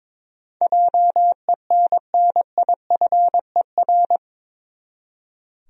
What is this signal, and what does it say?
Morse code, heard as a keyed tone, about 22 wpm, 700 Hz. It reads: JENNIFER
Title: USA Common Names - 22wpm